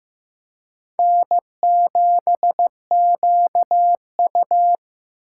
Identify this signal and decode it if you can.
Morse code, heard as a keyed tone, about 18 wpm, 700 Hz. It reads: N7QU
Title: Speed Racing - USA 1x2 Call Signs 15wpm